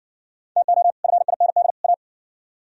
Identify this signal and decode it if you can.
Morse code, heard as a keyed tone, about 60 wpm, 700 Hz. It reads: TO FIND A